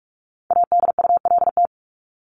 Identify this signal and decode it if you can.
Morse code, heard as a keyed tone, about 44 wpm, 700 Hz. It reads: ADULT
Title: Top 1501-1750 Words 45wpm